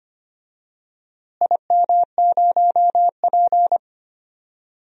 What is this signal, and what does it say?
Morse code, heard as a keyed tone, about 25 wpm, 700 Hz. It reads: IM0P
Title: DXCC-100 Call Signs - 4 Characters - Repeated 3x - 25wpm